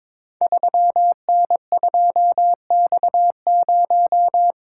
Morse code, heard as a keyed tone, about 22 wpm, 700 Hz. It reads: 3N2X0
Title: Code Groups - 5 Letter:Numbers - 22wpm